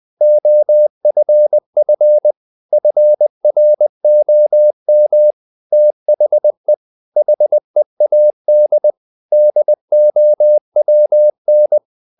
Morse code, heard as a keyed tone, about 20 wpm, 600 Hz. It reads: OFF FROM THE HEAD DOWN
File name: the old man and the sea part 2 20wpm